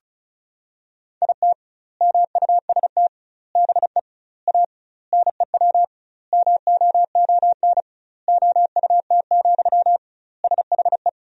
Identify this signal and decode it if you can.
Morse code, heard as a keyed tone, about 36 wpm, 700 Hz. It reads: IT MUST BE A NEW MOON OUT, SHE
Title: 2x Word-Spacing - Sentences from Top 500 Words - 35wpm